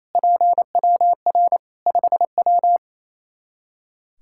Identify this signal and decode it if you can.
Morse code, heard as a keyed tone, about 27 wpm, 700 Hz. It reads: PWR 5W
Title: Practice QSO - Exchange 2 - Info - 28wpm